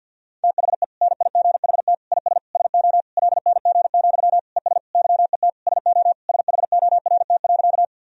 Morse code, heard as a keyed tone, about 48 wpm, 700 Hz. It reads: THE NIGHT IS SO LNG, ES YET SO SHORT.
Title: Sentences from Top 500 Words and Abbreviations 50wpm